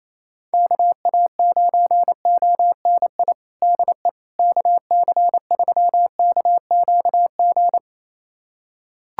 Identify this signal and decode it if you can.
Morse code, heard as a keyed tone, about 28 wpm, 700 Hz. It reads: KA9ONI DE KC3KQG
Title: USA Callsign DE Callsign - 6 Characters - 28wpm